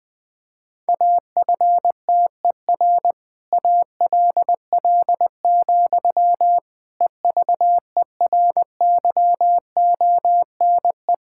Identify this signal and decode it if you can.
Morse code, heard as a keyed tone, about 20 wpm, 700 Hz. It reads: AFTER ALL, EVERYONE
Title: Sentences from Top 1000 Words - Review Part 5 - 20wpm